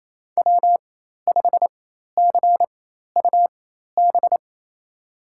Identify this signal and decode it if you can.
Morse code, heard as a keyed tone, about 28 wpm, spaced as for 13 wpm, 700 Hz. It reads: W5CUB
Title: USA Call Signs - 4x Character Spacing - 28wpm